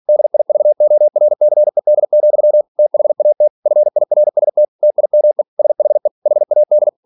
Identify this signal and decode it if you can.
Morse code, heard as a keyed tone, about 46 wpm, 600 Hz. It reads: DIVORCED, THAT FIRST TIME SHE HAD
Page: Never 47wpm